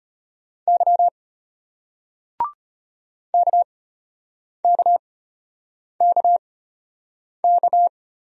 Morse code, heard as a keyed tone, about 34 wpm, spaced as for 8 wpm, 700 Hz. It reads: Y KKKK